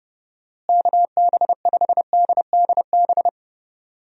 Morse code, heard as a keyed tone, about 29 wpm, 700 Hz. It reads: KB5DDB
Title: USA Call Signs - 6 Characters - Repeated 2x - 30wpm